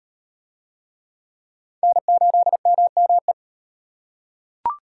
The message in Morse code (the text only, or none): N8MME